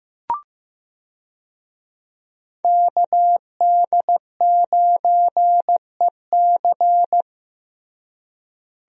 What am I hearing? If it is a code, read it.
Morse code, heard as a keyed tone, about 15 wpm, 700 Hz. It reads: KD9EC
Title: USA Call Signs - 5 Characters - Repeated 2x - 15wpm